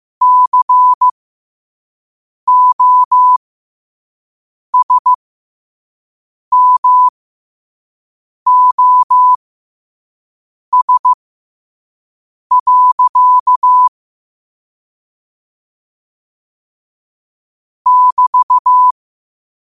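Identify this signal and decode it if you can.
Morse code, heard as a keyed tone, about 15 wpm, spaced as for 5 wpm, 1000 Hz. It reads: COSMOS. <BT>
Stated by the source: Morse sent at 5 wpm.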